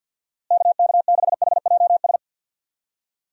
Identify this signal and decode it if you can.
Morse code, heard as a keyed tone, about 48 wpm, 700 Hz. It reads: KK6HJS
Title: USA Call Signs - 6 Characters - Repeated 2x - 50wpm